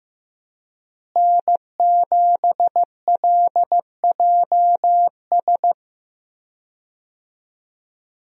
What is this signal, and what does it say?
Morse code, heard as a keyed tone, about 15 wpm, 700 Hz. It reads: N7LJS
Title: USA Call Signs - 5 Characters - Repeated 2x - 15wpm